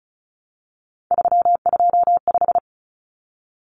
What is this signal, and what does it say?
Morse code, heard as a keyed tone, about 34 wpm, 700 Hz. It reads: 325
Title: Numbers - 3-Digit 35wpm